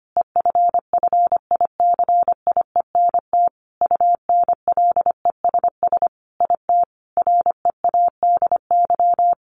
Morse code, heard as a keyed tone, about 25 wpm, 700 Hz. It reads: EFFICIENT UNLESS IT READY
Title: Sentences from Top 1300 Words 25wpm